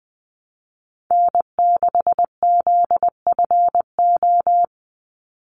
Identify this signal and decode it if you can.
Morse code, heard as a keyed tone, about 20 wpm, 700 Hz. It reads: N6ZFO
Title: Call Signs - 20wpm